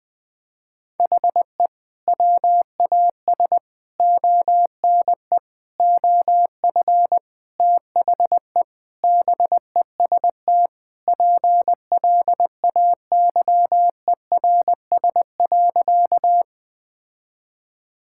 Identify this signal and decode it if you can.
Morse code, heard as a keyed tone, about 21 wpm, 700 Hz. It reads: HE WAS ONE OF THE BEST PLAYERS.
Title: Sentences with 5-Word Phrases - 20wpm